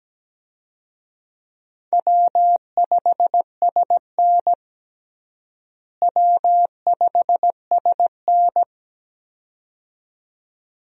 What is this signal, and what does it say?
Morse code, heard as a keyed tone, about 17 wpm, 700 Hz. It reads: W5SN W5SN
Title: USA Call Signs - 4 Characters - Repeated 2x - 17wpm